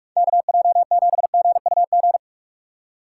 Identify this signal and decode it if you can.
Morse code, heard as a keyed tone, about 42 wpm, 700 Hz. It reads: KJ7GUG
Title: USA Call Signs - 6 Characters - 2x Character Spacing - 45wpm